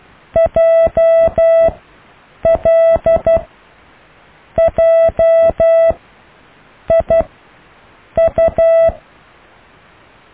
Morse code, heard as a keyed tone, about 12 wpm, spaced as for 6 wpm, 660 Hz. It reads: JLJIU
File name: cw / IJLMU